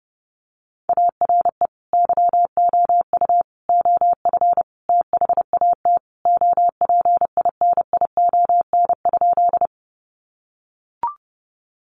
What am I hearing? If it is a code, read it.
Morse code, heard as a keyed tone, about 28 wpm, 700 Hz. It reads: ARE YOU OF THAT OPINION?
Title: Sentences from Top 1200 Words 30wpm